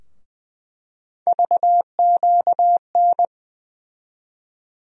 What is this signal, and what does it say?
Morse code, heard as a keyed tone, about 20 wpm, 700 Hz. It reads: VQN
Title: USA Call Sign Postfixes - 3 Letters 20wpm